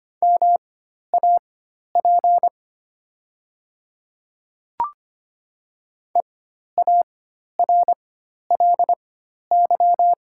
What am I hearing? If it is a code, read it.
Morse code, heard as a keyed tone, about 25 wpm, spaced as for 12 wpm, 700 Hz. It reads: MAP EARLY
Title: 4x Character Spacing - Top 500 Words - 25wpm